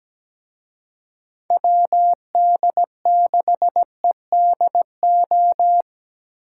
Morse code, heard as a keyed tone, about 20 wpm, 700 Hz. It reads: WD6EDO
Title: Speed Racing - USA 2x3 Call Signs 17wpm